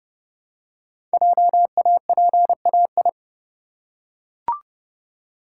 Morse code, heard as a keyed tone, about 30 wpm, 690 Hz. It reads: JAPAI
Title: Code Groups - 5 Letter:Numbers - 30wpm